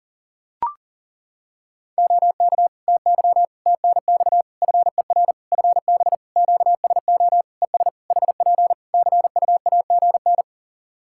Morse code, heard as a keyed tone, about 40 wpm, 700 Hz. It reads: OK TY TNX FER FB QSO ES HP CUAGN
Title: K1USN SST QSO Practice 40wpm